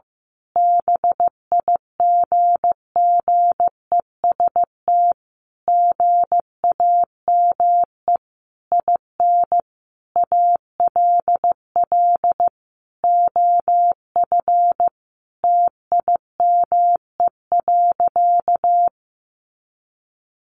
Morse code, heard as a keyed tone, about 15 wpm, 700 Hz. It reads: BIGGEST GAME IN ALL OF TIME.
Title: A Sound Of Thunder 15wpm